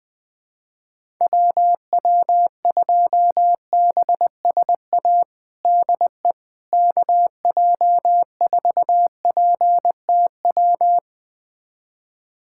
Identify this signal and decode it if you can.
Morse code, heard as a keyed tone, about 20 wpm, 700 Hz. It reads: WW2BSA DE KJ4PTW
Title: USA Callsign DE Callsign - 6 Characters - 20wpm